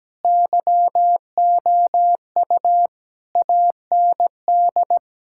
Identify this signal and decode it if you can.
Morse code, heard as a keyed tone, about 17 wpm, 700 Hz. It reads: YOU AND
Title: Sentences from Top 1600 Words 17wpm